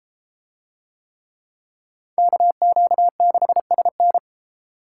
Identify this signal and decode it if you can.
Morse code, heard as a keyed tone, about 31 wpm, 700 Hz. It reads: KQ6SN